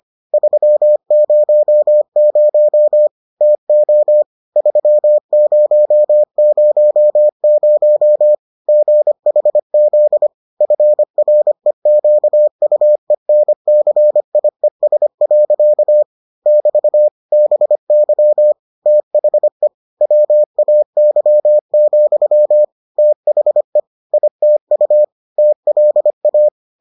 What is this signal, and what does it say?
Morse code, heard as a keyed tone, about 25 wpm, 600 Hz. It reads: 300 TO 3000 GHZ FREQUENCIES. <BT> BY THE WAY, THE ITU TLA